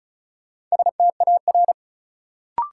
Morse code, heard as a keyed tone, about 36 wpm, 700 Hz. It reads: STAR
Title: Sentences from Top 1000 Words - Review Part 2 - New Word Drill - 35wpm